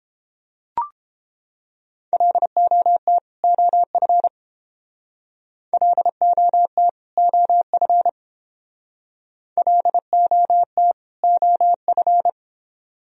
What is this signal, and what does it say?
Morse code, heard as a keyed tone, about 28 wpm, 700 Hz. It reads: LOT OF LOT OF LOT OF